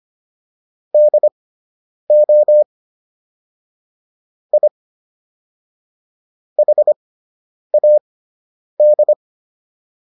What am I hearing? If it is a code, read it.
Morse code, heard as a keyed tone, about 25 wpm, spaced as for 9 wpm, 600 Hz. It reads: DO I HAD